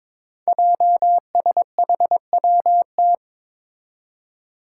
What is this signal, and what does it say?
Morse code, heard as a keyed tone, about 22 wpm, 700 Hz. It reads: JSHWT
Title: Five Challenging Letter-Number Combinations 22wpm